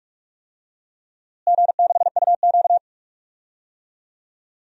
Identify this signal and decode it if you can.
Morse code, heard as a keyed tone, about 41 wpm, 700 Hz. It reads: G6UQ
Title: DXCC-100 Call Signs - 4 Characters - Repeated 3x - 45wpm